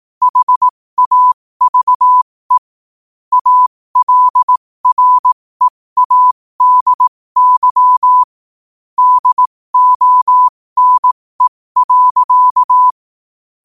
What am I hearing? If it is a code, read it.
Morse code, heard as a keyed tone, about 18 wpm, spaced as for 15 wpm, 1000 Hz. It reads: HAVE ALREADY DONE.
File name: qotdfiles / 15wpm